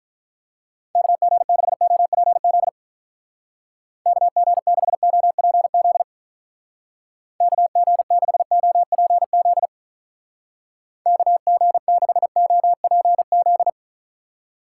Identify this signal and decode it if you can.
Morse code, heard as a keyed tone, about 45 wpm, 700 Hz. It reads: KG6OPZ KG6OPZ KG6OPZ KG6OPZ